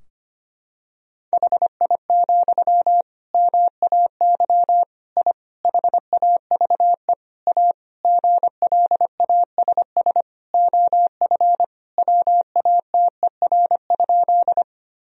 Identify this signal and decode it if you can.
Morse code, heard as a keyed tone, about 25 wpm, 700 Hz. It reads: HI, MAY I HAVE A GLASS OF WATER?
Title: Sentences from Top 1000 Words - Review Part 3 - 25wpm